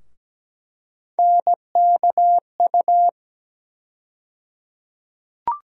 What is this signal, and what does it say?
Morse code, heard as a keyed tone, about 17 wpm, 700 Hz. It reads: NKU
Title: USA Call Sign Postfixes - 3 Letters 17wpm